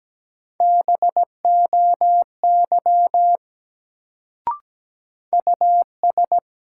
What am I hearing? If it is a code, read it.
Morse code, heard as a keyed tone, about 17 wpm, 700 Hz. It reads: BOY US
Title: Sentences from Top 200 Words - New Word Drill - 17wpm